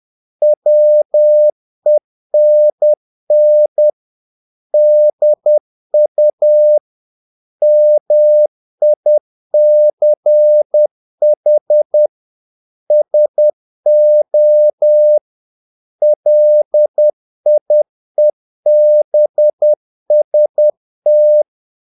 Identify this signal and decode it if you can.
Morse code, heard as a keyed tone, about 10 wpm, 600 Hz. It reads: WENN DU MICH SO LIEBST
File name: dk5ke / Humor-50BpM